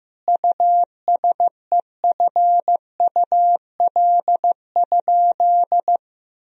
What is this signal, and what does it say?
Morse code, heard as a keyed tone, about 15 wpm, 700 Hz. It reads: USEFUL?
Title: Sentences from Top 1000 Words - Review Part 4 - 15wpm